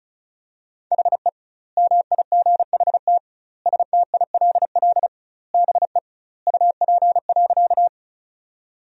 Morse code, heard as a keyed tone, about 36 wpm, 700 Hz. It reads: HE MIGHT STILL BE UP.
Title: 2x Word-Spacing - Sentences from Top 300 Words - 35wpm